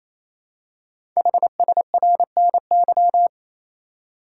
Morse code, heard as a keyed tone, about 27 wpm, 700 Hz. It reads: HSRNY
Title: Code Groups - 5 Letters - 28wpm